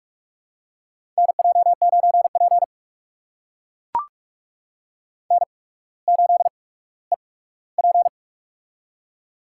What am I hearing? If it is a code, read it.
Morse code, heard as a keyed tone, about 44 wpm, spaced as for 19 wpm, 690 Hz. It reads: NJ9P N8EP
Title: USA Call Signs - 4 Characters - 8x Character Spacing - 45wpm